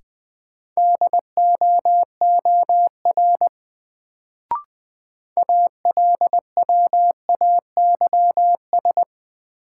Morse code, heard as a keyed tone, about 20 wpm, 700 Hz. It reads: DOOR ALWAYS
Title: Sentences from Top 300 Words - New Word Drill - 20wpm